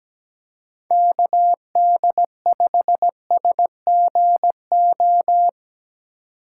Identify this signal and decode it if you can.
Morse code, heard as a keyed tone, about 17 wpm, 700 Hz. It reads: KD5SGO
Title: USA Call Signs - 6 Characters - 4x Character Spacing - 17wpm